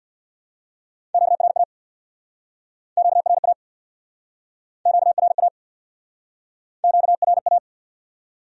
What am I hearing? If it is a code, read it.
Morse code, heard as a keyed tone, about 66 wpm, 700 Hz. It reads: QRA QRA QRA QRA